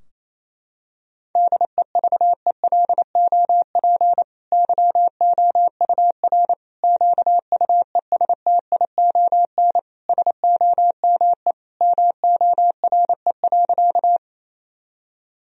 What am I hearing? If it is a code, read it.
Morse code, heard as a keyed tone, about 27 wpm, 700 Hz. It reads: DEVELOP YOUR QUESTION SOME MORE.
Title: Sentences from Top 1000 Words - Review Part 2 - 28wpm